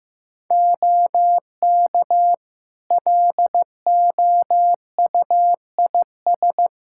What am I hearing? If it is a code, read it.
Morse code, heard as a keyed tone, about 15 wpm, 700 Hz. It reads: OK LOUIS
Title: Practice QSO - Exchange 3 - More Info - 15wpm